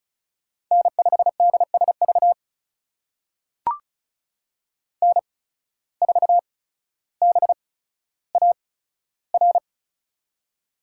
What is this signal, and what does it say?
Morse code, heard as a keyed tone, about 36 wpm, spaced as for 15 wpm, 700 Hz. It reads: N5DSV N4BAR
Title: USA Call Signs - 5 Characters - 8x Character Spacing - 35wpm